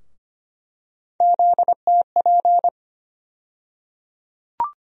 Morse code, heard as a keyed tone, about 25 wpm, 700 Hz. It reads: ZTP
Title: USA Call Sign Postfixes - 3 Letters 25wpm